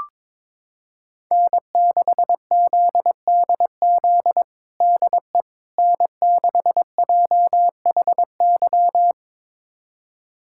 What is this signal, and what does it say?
Morse code, heard as a keyed tone, about 22 wpm, 700 Hz. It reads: N6ZDZ DE N6JHY
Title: USA Callsign DE Callsign - 5 Characters - 22wpm